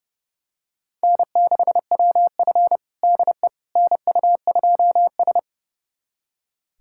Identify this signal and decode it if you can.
Morse code, heard as a keyed tone, about 30 wpm, 700 Hz. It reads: N6WF DE NU2S
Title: USA Callsign DE Callsign - 4 Characters - 30wpm